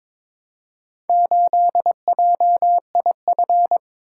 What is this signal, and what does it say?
Morse code, heard as a keyed tone, about 22 wpm, 700 Hz. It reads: 8JIF
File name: practice / 22wpm / Four Letter-Number Combinations - 22wpm